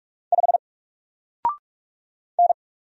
Morse code, heard as a keyed tone, about 46 wpm, 700 Hz. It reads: 5 N
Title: Single Character (Letter:Number:Punctuation:Prosign) Alt 45wpm